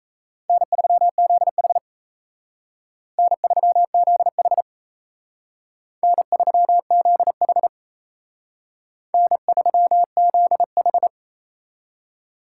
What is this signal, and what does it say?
Morse code, heard as a keyed tone, about 35 wpm, 700 Hz. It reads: N3ZH N3ZH N3ZH N3ZH
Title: Speed Racing - USA Call Signs 28wpm